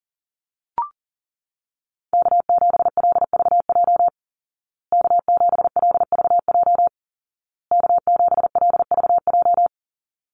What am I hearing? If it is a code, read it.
Morse code, heard as a keyed tone, about 40 wpm, 700 Hz. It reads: K7LVJ K7LVJ K7LVJ
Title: Call Signs Repeated 2x - 40wpm